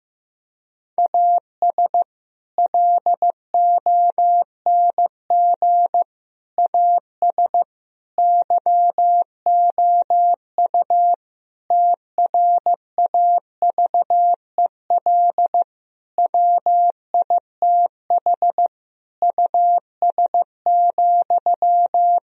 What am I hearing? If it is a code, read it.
Morse code, heard as a keyed tone, about 15 wpm, 700 Hz. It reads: AS LONG AS YOU TRAVEL WITH US,
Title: Sentences from Top 1000 Words - Review Part 2 - 15wpm